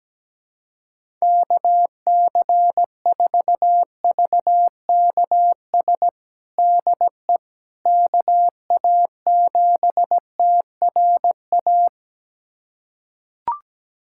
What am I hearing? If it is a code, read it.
Morse code, heard as a keyed tone, about 17 wpm, 700 Hz. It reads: KC4VKS DE KA7TRA
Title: USA Callsign DE Callsign 17wpm